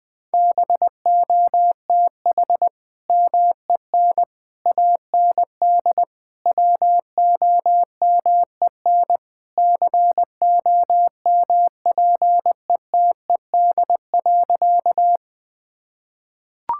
Morse code, heard as a keyed tone, about 21 wpm, 700 Hz. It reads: BOTH MEN AND WOMEN COMPETED.
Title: Sentences with 3-Word Phrases - 20wpm